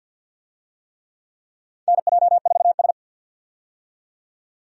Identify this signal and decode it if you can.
Morse code, heard as a keyed tone, about 52 wpm, 700 Hz. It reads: NJ4S
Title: USA Call Signs - 4 Characters - 50wpm